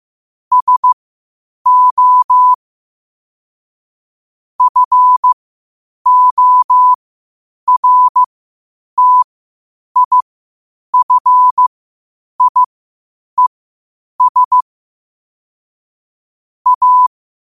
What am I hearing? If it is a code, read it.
Morse code, heard as a keyed tone, about 15 wpm, spaced as for 8 wpm, 1000 Hz. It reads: SO FORTIFIES A
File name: qotdfiles / 8wpm